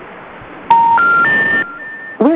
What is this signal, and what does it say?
Special information tone.
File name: recordings / REd e5c2b76e005aa19510d428d0f